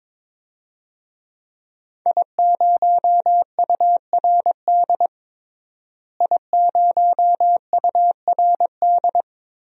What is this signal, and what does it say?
Morse code, heard as a keyed tone, about 22 wpm, 700 Hz. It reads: I0URD I0URD
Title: DXCC-100 Call Signs - 5 Characters - Repeated 3x - 22wpm